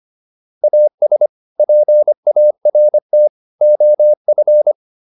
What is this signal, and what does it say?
Morse code, heard as a keyed tone, about 25 wpm, 600 Hz. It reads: AS PART OF